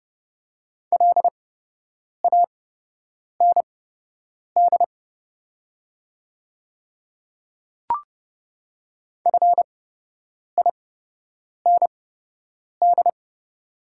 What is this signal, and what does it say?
Morse code, heard as a keyed tone, about 29 wpm, spaced as for 8 wpm, 700 Hz. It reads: LAND FIND
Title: 8x Character Spacing - Top 200 Words - 30wpm